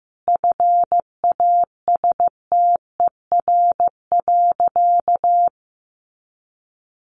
Morse code, heard as a keyed tone, about 15 wpm, 700 Hz. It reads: FASTER.
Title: Additional Easy Phrases 15wpm